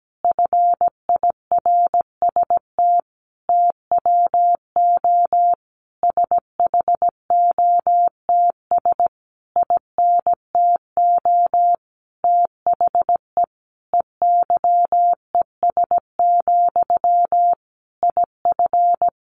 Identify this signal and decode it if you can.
Morse code, heard as a keyed tone, about 17 wpm, 700 Hz. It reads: FIRST TWO SHOTS INTO THE EYES, IF
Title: A Sound Of Thunder 17wpm